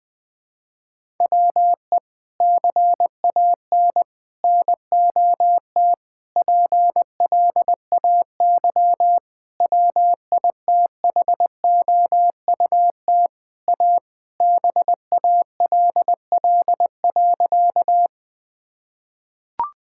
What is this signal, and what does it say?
Morse code, heard as a keyed tone, about 20 wpm, 700 Hz. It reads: WE CAN NOT PLAY WITHOUT A BALL.
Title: Sentences from Top 1000 Words - Review Part 6 - 20wpm